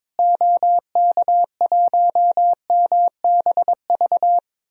Morse code, heard as a keyed tone, about 21 wpm, 700 Hz. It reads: OK1MBV
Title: DXCC-100 Call Signs - 6 Characters - Repeated 3x - 22wpm